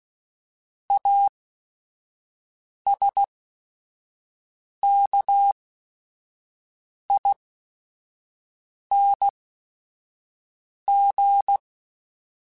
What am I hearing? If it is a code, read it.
Morse code, heard as a keyed tone, about 16 wpm, spaced as for 5 wpm, 800 Hz. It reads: ASKING
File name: Special INGs / Special INGs-mp3s / Sngl Ing @5